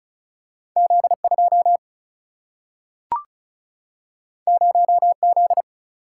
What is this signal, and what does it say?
Morse code, heard as a keyed tone, about 36 wpm, 700 Hz. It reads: Z2 0Z
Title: Single-Letter Single-Number 35wpm